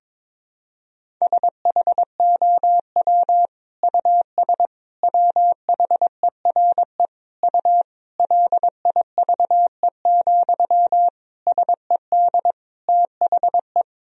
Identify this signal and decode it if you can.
Morse code, heard as a keyed tone, about 21 wpm, 700 Hz. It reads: SHOW US WHERE U LIVE, SED THE
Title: Sentences from Top 200 Words and Abbreviations 22wpm